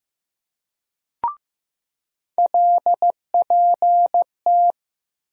Morse code, heard as a keyed tone, about 15 wpm, 700 Hz. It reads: LPT
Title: USA Call Sign Postfixes - 3 Letters 15wpm